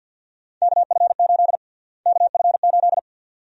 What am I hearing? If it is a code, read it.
Morse code, heard as a keyed tone, about 52 wpm, 700 Hz. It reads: KF8 KF8
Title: USA Call Sign Prefixes - Repeated 2x - 50wpm